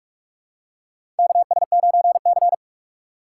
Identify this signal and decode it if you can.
Morse code, heard as a keyed tone, about 42 wpm, 700 Hz. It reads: KS9C
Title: USA Call Signs - 4 Characters - 4x Character Spacing - 45wpm